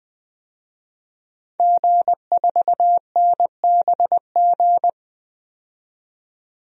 Morse code, heard as a keyed tone, about 20 wpm, 700 Hz. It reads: G4NBG
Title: DXCC-100 Call Signs - 5 Characters - 20wpm